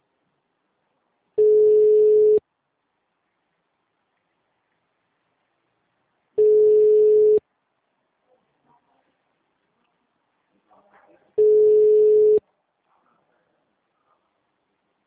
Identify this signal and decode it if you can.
Ringback tone (Europe, 425 Hz), with speech.